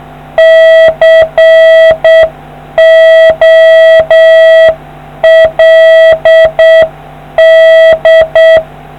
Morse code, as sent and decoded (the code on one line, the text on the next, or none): -.-. --- .-.. -..
COLD